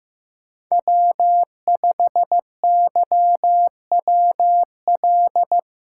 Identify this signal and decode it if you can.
Morse code, heard as a keyed tone, about 19 wpm, 700 Hz. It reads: W5YWL